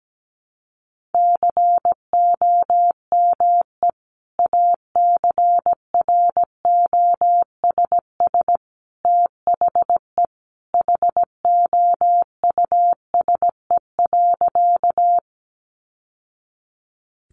Morse code, heard as a keyed tone, about 17 wpm, 700 Hz. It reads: COME ACROSS THE HOUSE.
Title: Sentences from Top 700 Words 17wpm